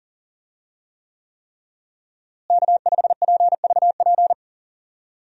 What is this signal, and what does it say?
Morse code, heard as a keyed tone, about 40 wpm, 700 Hz. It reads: K5PVP
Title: USA Call Signs - 5 Characters - Repeated 3x - 40wpm